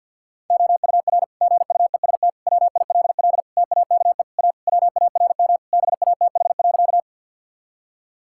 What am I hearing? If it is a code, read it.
Morse code, heard as a keyed tone, about 52 wpm, 700 Hz. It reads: OUR GUEST WILL TAKE A WARM BATH.
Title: Sentences from Top 1600 Words 50wpm